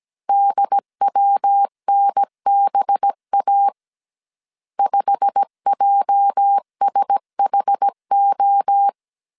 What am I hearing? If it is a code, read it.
Morse code, heard as a keyed tone, about 17 wpm, 800 Hz. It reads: DWNBA 5JSHO
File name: CW / DET / 1 9 1500C S A 16wpm 2828